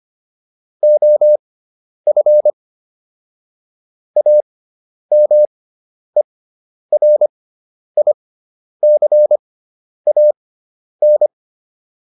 Morse code, heard as a keyed tone, about 25 wpm, spaced as for 10 wpm, 600 Hz. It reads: OF AMERICAN